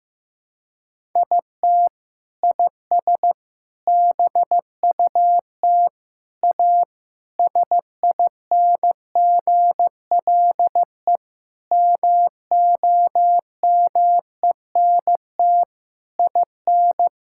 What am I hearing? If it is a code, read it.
Morse code, heard as a keyed tone, about 15 wpm, 700 Hz. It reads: IT IS BUT A SINGLE MOMENT IN
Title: Sentences from Top 1000 Words - Review Part 6 - 15wpm